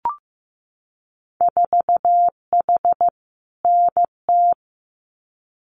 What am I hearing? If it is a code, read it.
Morse code, heard as a keyed tone, about 15 wpm, 700 Hz. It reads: 4H NT
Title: Winter Field Day Exchanges Encore 15wpm